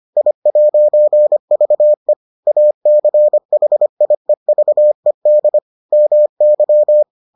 Morse code, heard as a keyed tone, about 25 wpm, 600 Hz. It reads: I'VE ACHIEVED MY